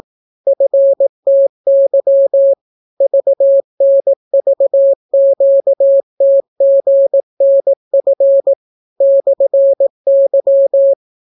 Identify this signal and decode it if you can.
Morse code, heard as a keyed tone, about 18 wpm, 550 Hz. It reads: FTY VNVQTGNF /Y